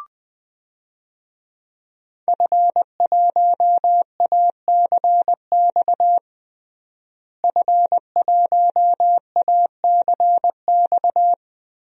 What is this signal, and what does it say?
Morse code, heard as a keyed tone, about 20 wpm, 700 Hz. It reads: F1ACX F1ACX
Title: DXCC-100 Call Signs - 5 Characters - Repeated 3x - 20wpm